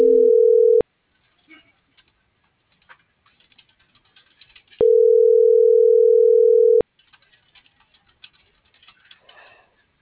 Ringback tone (North America), 440 + 480 Hz.